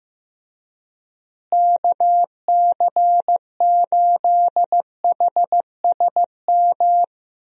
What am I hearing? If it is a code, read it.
Morse code, heard as a keyed tone, about 16 wpm, 700 Hz. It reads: KC8HSM
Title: USA Call Signs - 6 Characters - Repeated 2x - 15wpm